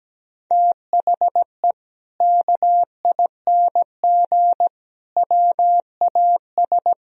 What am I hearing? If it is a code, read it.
Morse code, heard as a keyed tone, about 17 wpm, 700 Hz. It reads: THE KING WAS
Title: Sentences from Top 1000 Words - Review Part 3 - 17wpm